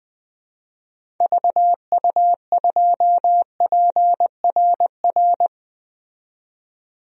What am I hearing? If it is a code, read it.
Morse code, heard as a keyed tone, about 21 wpm, 700 Hz. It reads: VU2PRR
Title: DXCC-100 Call Signs - 6 Characters - 20wpm